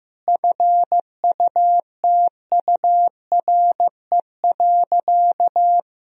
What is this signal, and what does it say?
Morse code, heard as a keyed tone, about 15 wpm, 700 Hz. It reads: FUTURE.
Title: Sentences from Top 1000 Words - Review Part 4 - 15wpm